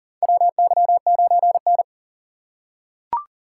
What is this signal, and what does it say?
Morse code, heard as a keyed tone, about 40 wpm, 700 Hz. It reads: WY9N
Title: USA Call Signs - 4 Characters - 6x Character Spacing - 40wpm